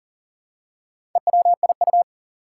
Morse code, heard as a keyed tone, about 39 wpm, 700 Hz. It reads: EWIU
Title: Four Challenging Letter-Number Combinations 40wpm